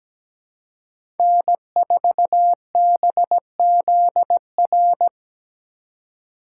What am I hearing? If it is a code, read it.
Morse code, heard as a keyed tone, about 17 wpm, 700 Hz. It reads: N4BZR
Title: USA Call Signs - 5 Characters - 2x Character Spacing - 17wpm